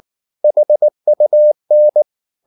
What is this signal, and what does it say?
Morse code, heard as a keyed tone, about 19 wpm, 600 Hz. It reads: HUN
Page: Ferie 19 wpm